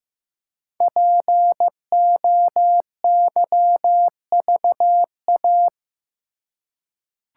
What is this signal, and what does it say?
Morse code, heard as a keyed tone, about 15 wpm, 700 Hz. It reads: POYVA
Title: Code Groups - 5 Letters - 15wpm